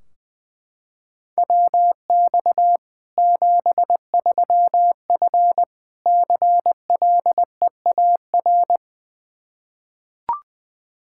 Morse code, heard as a keyed tone, about 21 wpm, 700 Hz. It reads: WX 73F CLEAR
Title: Practice QSO - Exchange 2 - Info - 20wpm